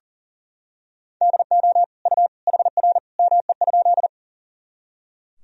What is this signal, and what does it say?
Morse code, heard as a keyed tone, about 40 wpm, 700 Hz. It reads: DO U HR ME?
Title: Sentences from Top 300 Words and Abbreviations 40wpm